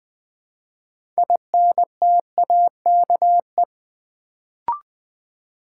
Morse code, heard as a keyed tone, about 20 wpm, 700 Hz. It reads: INTAKE
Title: Build A Word - 3 Letters At A Time - 6-Letter Words - 20wpm